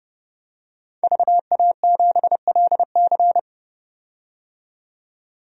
Morse code, heard as a keyed tone, about 30 wpm, 700 Hz. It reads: VA7LC
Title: DXCC-100 Call Signs - 5 Characters - 30wpm